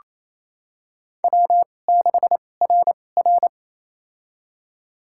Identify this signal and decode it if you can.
Morse code, heard as a keyed tone, about 27 wpm, spaced as for 20 wpm, 700 Hz. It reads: W6RR
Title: USA Call Signs - 4 Characters - 2x Character Spacing - 28wpm